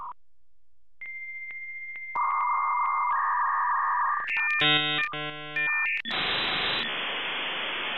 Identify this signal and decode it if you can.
Dial-up modem handshake.